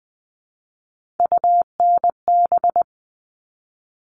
Morse code, heard as a keyed tone, about 20 wpm, 700 Hz. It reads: UNB
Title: All Cut Numbers - 3-Digit 20wpm